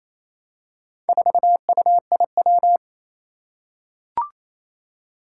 Morse code, heard as a keyed tone, about 28 wpm, 700 Hz. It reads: 4UIW
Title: Four Challenging Letter-Number Combinations 28wpm